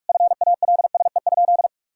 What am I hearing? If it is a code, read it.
Morse code, heard as a keyed tone, about 46 wpm, 700 Hz. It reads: FALSE?
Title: Sentences from Top 1600 Words 45wpm